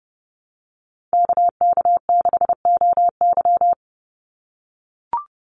Morse code, heard as a keyed tone, about 30 wpm, 700 Hz. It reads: KK6OY
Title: USA Call Signs 30wpm